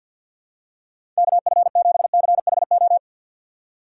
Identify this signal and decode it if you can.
Morse code, heard as a keyed tone, about 52 wpm, 700 Hz. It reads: KF7CHO
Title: USA Call Signs - 6 Characters - 4x Character Spacing - 50wpm